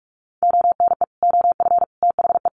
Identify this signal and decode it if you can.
Morse code, heard as a keyed tone, about 44 wpm, 700 Hz. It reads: ONE OF THE
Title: Sets of 5 Words 45wpm